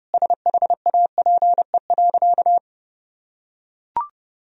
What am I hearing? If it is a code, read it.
Morse code, heard as a keyed tone, about 32 wpm, 700 Hz. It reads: SHAPE.
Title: Sentences from Top 1000 Words - Review Part 3 - 30wpm